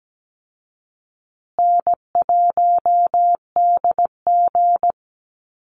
Morse code, heard as a keyed tone, about 17 wpm, 700 Hz. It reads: N1DG
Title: Call Signs Repeated 2x - 17wpm